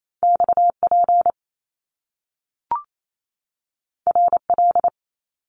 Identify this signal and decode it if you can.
Morse code, heard as a keyed tone, about 28 wpm, 700 Hz. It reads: XP RL
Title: Top Two Letter Combinations in Words 28wpm